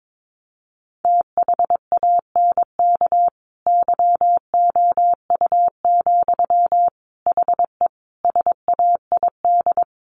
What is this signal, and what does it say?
Morse code, heard as a keyed tone, about 22 wpm, 700 Hz. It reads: THANK YOU, HE SAID
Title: Sentences from Top 1500 Words 22wpm